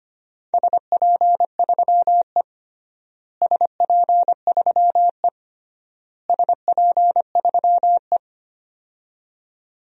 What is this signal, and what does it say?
Morse code, heard as a keyed tone, about 25 wpm, 700 Hz. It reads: SP3E SP3E SP3E
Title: DXCC-100 Call Signs - 4 Characters - Repeated 3x - 25wpm